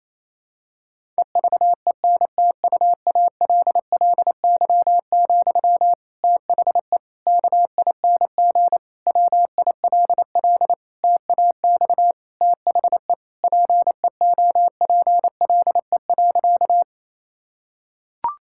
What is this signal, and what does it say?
Morse code, heard as a keyed tone, about 28 wpm, 700 Hz. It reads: EVENTUALLY, THE KING WILL TAX THE PEOPLE.
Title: Sentences from Top 1000 Words - Review Part 6 - 28wpm